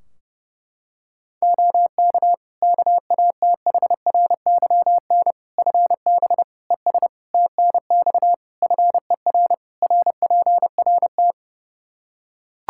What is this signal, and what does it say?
Morse code, heard as a keyed tone, about 30 wpm, 700 Hz. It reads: OK KATHRYN FB ES TNX FER RPRT
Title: Practice QSO - Exchange 2 - Info - 30wpm